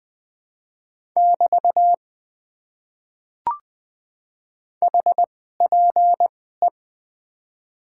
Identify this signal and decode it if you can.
Morse code, heard as a keyed tone, about 21 wpm, 700 Hz. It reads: <BT> H P E
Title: Top 90 QSO Elements - 2x Character Spacing - 20wpm